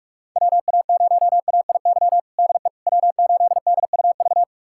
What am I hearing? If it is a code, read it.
Morse code, heard as a keyed tone, about 46 wpm, 700 Hz. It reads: WA0AIY DE W8DUV
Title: USA Callsign DE Callsign 45wpm